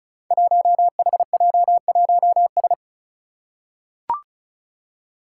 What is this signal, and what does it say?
Morse code, heard as a keyed tone, about 36 wpm, 700 Hz. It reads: 1HJ1S
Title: Five Challenging Letter-Number Combinations 35wpm